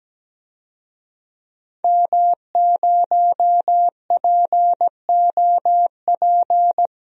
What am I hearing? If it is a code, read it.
Morse code, heard as a keyed tone, about 17 wpm, 700 Hz. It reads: M0POP
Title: DXCC-100 Call Signs - 5 Characters - 17wpm